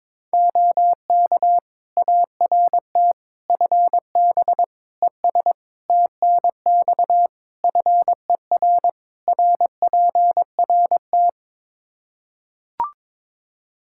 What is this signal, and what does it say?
Morse code, heard as a keyed tone, about 22 wpm, 700 Hz. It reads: OK ART FB ES TNX FER RPRT
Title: Practice QSO - Exchange 2 - Info - 22wpm